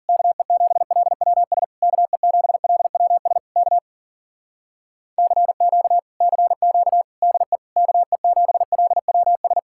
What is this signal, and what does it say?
Morse code, heard as a keyed tone, about 52 wpm, 700 Hz. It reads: KE7LWS KE7LWS K CQ CQ DE KE7LWS